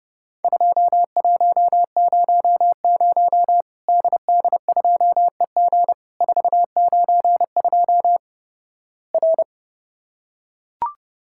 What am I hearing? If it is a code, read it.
Morse code, heard as a keyed tone, about 30 wpm, 700 Hz. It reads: 2100 DD2EG 492 R
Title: Worked All Europe DX Contest - QTC Exchange - 30wpm